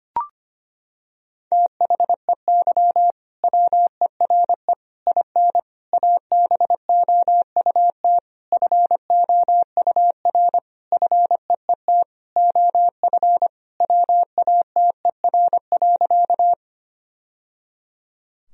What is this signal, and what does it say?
Morse code, heard as a keyed tone, about 25 wpm, 700 Hz. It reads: THEY WERE IN ABOUT FOUR FEET OF WATER.
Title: Sentences from Top 1000 Words - Review Part 1 - 25wpm